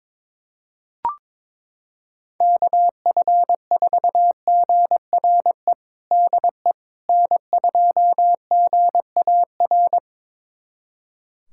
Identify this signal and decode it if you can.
Morse code, heard as a keyed tone, about 22 wpm, 700 Hz. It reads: KF4GRE DE N2GAR
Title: USA Callsign DE Callsign 22wpm